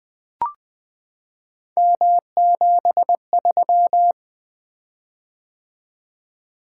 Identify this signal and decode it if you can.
Morse code, heard as a keyed tone, about 20 wpm, 700 Hz. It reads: M73
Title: Three Letter-Number Combinations - 20wpm